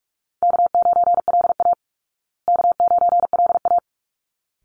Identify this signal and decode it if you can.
Morse code, heard as a keyed tone, about 44 wpm, 700 Hz. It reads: K9LA K9LA
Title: Easy Call Signs Repeated 2x - 45wpm